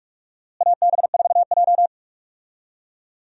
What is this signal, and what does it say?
Morse code, heard as a keyed tone, about 47 wpm, 700 Hz. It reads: AB4J
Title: USA Call Signs - 4 Characters - Repeated 2x - 45wpm